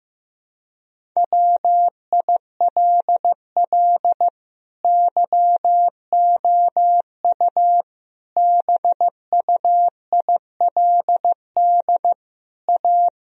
Morse code, heard as a keyed tone, about 15 wpm, 700 Hz. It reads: WILL YOU BUILD A
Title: Sentences from Top 1000 Words - Review Part 1 - 15wpm